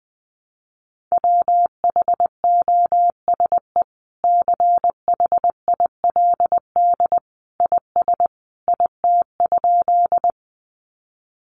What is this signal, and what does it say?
Morse code, heard as a keyed tone, about 20 wpm, 700 Hz. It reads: WHOSE CHILD IS IT?
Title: Sentences from Top 1200 Words 20wpm